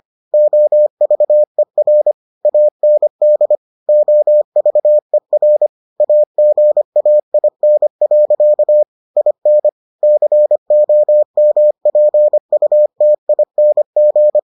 Morse code, heard as a keyed tone, about 25 wpm, 600 Hz. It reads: OVER AND OVER AGAIN. IN COMPUTING